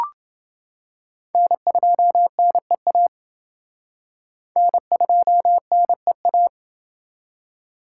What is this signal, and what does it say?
Morse code, heard as a keyed tone, about 29 wpm, 700 Hz. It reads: N2NEA N2NEA